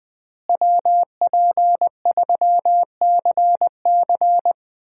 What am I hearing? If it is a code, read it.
Morse code, heard as a keyed tone, about 20 wpm, 700 Hz. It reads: WP3CC
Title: USA Call Signs - 5 Characters - Repeated 2x - 20wpm